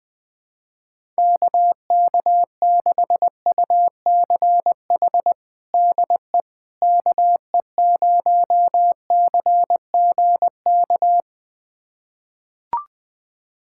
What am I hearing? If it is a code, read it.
Morse code, heard as a keyed tone, about 20 wpm, 700 Hz. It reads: KK6UCH DE KE0CGK
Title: USA Callsign DE Callsign 20wpm